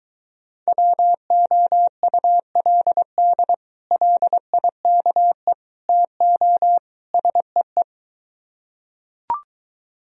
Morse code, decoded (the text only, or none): WOULD LIKE TO SEE